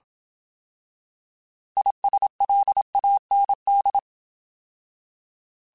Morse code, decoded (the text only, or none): ISLAND